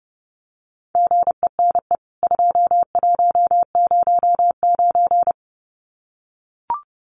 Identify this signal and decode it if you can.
Morse code, heard as a keyed tone, about 30 wpm, 700 Hz. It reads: GENE 2109
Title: CWT Exchange 30wpm